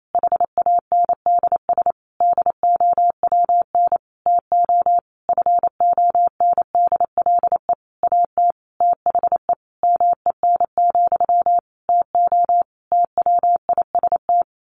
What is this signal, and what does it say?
Morse code, heard as a keyed tone, about 28 wpm, 700 Hz. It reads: HANDS DOWN TO FONDLE AT THE MEN, TO TWIST
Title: A Sound Of Thunder 28wpm